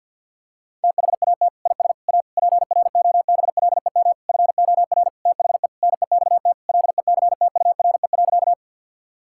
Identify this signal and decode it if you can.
Morse code, heard as a keyed tone, about 50 wpm, 700 Hz. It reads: THAT IS A PROBLEM FOR THE NEXT LECTURE.
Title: Sentences from Top 1600 Words 50wpm